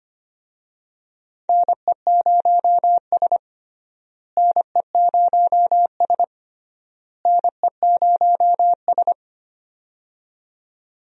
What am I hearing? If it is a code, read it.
Morse code, heard as a keyed tone, about 25 wpm, 700 Hz. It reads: NE0S NE0S NE0S
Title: USA Call Signs - 4 Characters - Repeated 3x - 25wpm